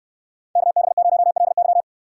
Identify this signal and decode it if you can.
Morse code, heard as a keyed tone, about 69 wpm, 700 Hz. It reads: KB0LY